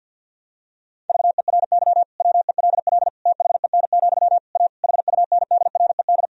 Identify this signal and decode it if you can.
Morse code, heard as a keyed tone, about 52 wpm, 700 Hz. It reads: VERY WELL THEN, A HUNDRED
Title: Sentences from Top 1000 Words - Review Part 3 - 50wpm